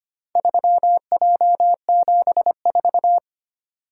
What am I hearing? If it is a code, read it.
Morse code, heard as a keyed tone, about 25 wpm, 700 Hz. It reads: 3J74
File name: practice / 25wpm / Four Letter-Number Combinations - 25wpm